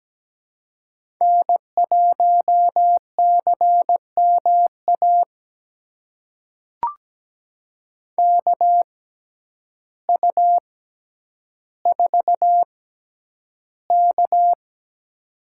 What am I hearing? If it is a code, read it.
Morse code, heard as a keyed tone, about 17 wpm, spaced as for 9 wpm, 700 Hz. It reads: N1CMA KU4K
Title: USA Call Signs - 6x Character Spacing - 17wpm